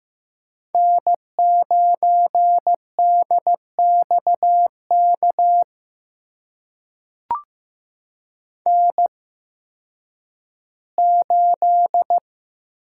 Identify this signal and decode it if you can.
Morse code, heard as a keyed tone, about 15 wpm, spaced as for 10 wpm, 700 Hz. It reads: N9DXK N8
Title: USA Call Signs - 5 Characters - 8x Character Spacing - 15wpm